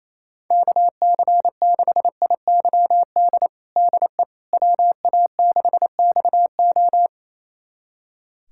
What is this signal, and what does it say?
Morse code, heard as a keyed tone, about 28 wpm, 700 Hz. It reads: KC6IYD DE WA6XO
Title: USA Callsign DE Callsign 28wpm